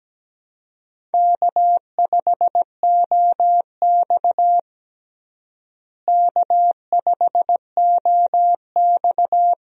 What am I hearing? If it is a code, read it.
Morse code, heard as a keyed tone, about 17 wpm, 700 Hz. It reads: K5OX K5OX
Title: USA Call Signs - 4 Characters - Repeated 3x - 17wpm